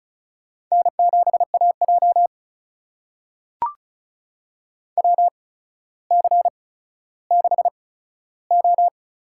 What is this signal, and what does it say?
Morse code, heard as a keyed tone, about 36 wpm, spaced as for 15 wpm, 700 Hz. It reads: N7AJ WC6O
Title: USA Call Signs - 4 Characters - 8x Character Spacing - 35wpm